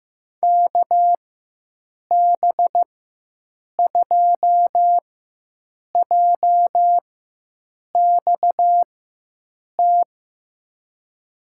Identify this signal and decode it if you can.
Morse code, heard as a keyed tone, about 15 wpm, spaced as for 7 wpm, 700 Hz. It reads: KB2JXT
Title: USA Call Signs - 6 Characters - 4x Character Spacing - 15wpm